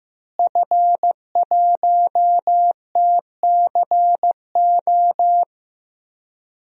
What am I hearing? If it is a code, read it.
Morse code, heard as a keyed tone, about 15 wpm, 700 Hz. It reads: F1TCO
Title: DXCC-100 Call Signs - 5 Characters - Repeated 3x - 15wpm